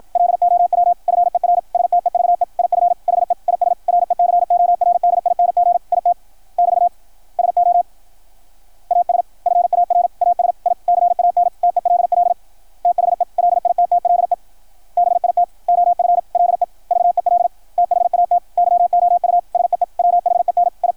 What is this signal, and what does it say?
Morse code, heard as a keyed tone, about 53 wpm, 700 Hz, with radio noise. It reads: LOW FER STEVE IF HE IS RECORDING IT <BT> SO AS FAR AS I CAN TELL THE LITTLE BIT OF LEVEL THAT YOU SEE WHEN I
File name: archived / CRSnet